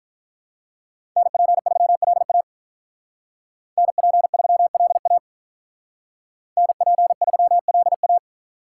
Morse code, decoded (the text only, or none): NP3LA NP3LA NP3LA